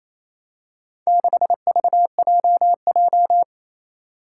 Morse code, heard as a keyed tone, about 28 wpm, 700 Hz. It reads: 6VJJ
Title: Four Challenging Letter-Number Combinations 28wpm